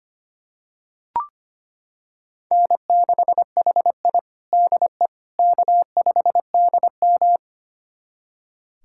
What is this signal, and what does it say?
Morse code, heard as a keyed tone, about 25 wpm, 700 Hz. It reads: N6HI DE K5DM
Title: USA Callsign DE Callsign - 4 Characters - 25wpm